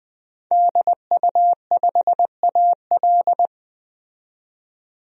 Morse code, heard as a keyed tone, about 21 wpm, 700 Hz. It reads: DU5AL
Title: DXCC-100 Call Signs - 5 Characters - 20wpm